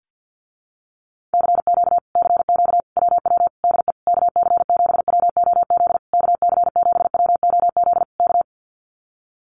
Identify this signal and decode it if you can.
Morse code, heard as a keyed tone, about 50 wpm, 700 Hz. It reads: CQ CQ WW DE KC7WOZ KC7WOZ K
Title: Common QSO Phrases Encore 50wpm